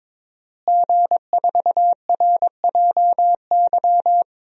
Morse code, heard as a keyed tone, about 21 wpm, 700 Hz. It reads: G4RJY
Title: DXCC-100 Call Signs - 5 Characters - Repeated 3x - 22wpm